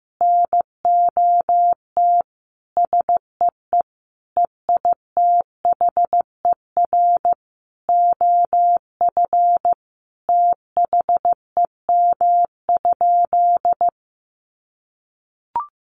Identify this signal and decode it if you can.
Morse code, heard as a keyed tone, about 15 wpm, 700 Hz. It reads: NOT SEE EITHER OF THEM?
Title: Sentences from Top 600 Words 15wpm